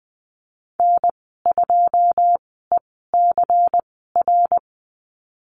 Morse code, heard as a keyed tone, about 20 wpm, spaced as for 14 wpm, 700 Hz. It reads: N2ECR
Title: 122 Lesson 20g - Number 3 - Call Signs 20wpm